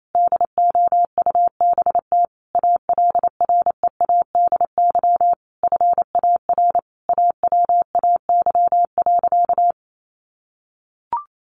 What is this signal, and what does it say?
Morse code, heard as a keyed tone, about 28 wpm, 700 Hz. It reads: DOUBT ALREADY FAR AWAY.
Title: Sentences from Top 1500 Words 28wpm